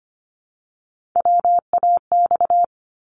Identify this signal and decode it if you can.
Morse code, heard as a keyed tone, about 25 wpm, 700 Hz. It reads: WAX
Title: Multi-letter Phonemes Repeated 3x 25wpm